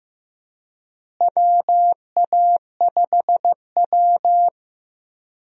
Morse code, heard as a keyed tone, about 15 wpm, 700 Hz. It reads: WA5W
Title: USA Call Signs - 4 Characters - Repeated 2x - 15wpm